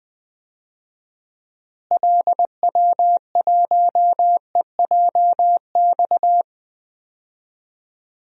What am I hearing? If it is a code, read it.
Morse code, heard as a keyed tone, about 20 wpm, 700 Hz. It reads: LW1EJX
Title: DXCC-100 Call Signs - 6 Characters - Repeated 2x - 20wpm